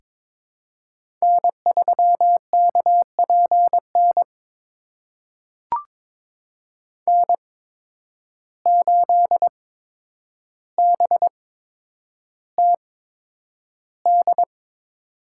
Morse code, heard as a keyed tone, about 22 wpm, spaced as for 9 wpm, 700 Hz. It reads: N3KPN N8BTD
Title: USA Call Signs - 5 Characters - 8x Character Spacing - 22wpm